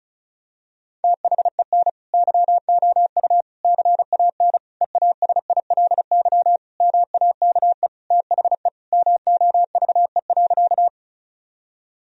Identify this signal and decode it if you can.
Morse code, heard as a keyed tone, about 36 wpm, 700 Hz. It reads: THEN YOU CAN EASILY MAKE THE MOVE.
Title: Sentences from Top 1000 Words - Review Part 1 - 35wpm